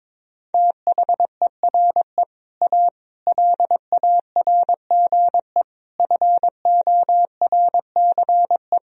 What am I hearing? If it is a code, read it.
Morse code, heard as a keyed tone, about 22 wpm, 700 Hz. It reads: THERE A LARGE FORCE
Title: Sentences from Top 1000 Words - Review Part 3 - 22wpm